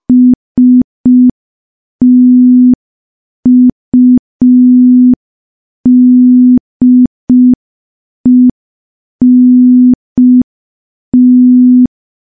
Morse code, heard as a keyed tone, about 5 wpm, 250 Hz. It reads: STUDENT
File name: morse / morse-training-05WPM